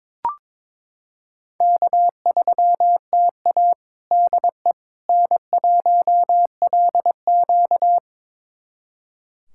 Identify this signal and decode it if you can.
Morse code, heard as a keyed tone, about 22 wpm, 700 Hz. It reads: K3TA DE N1LQ
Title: USA Callsign DE Callsign - 4 Characters - 22wpm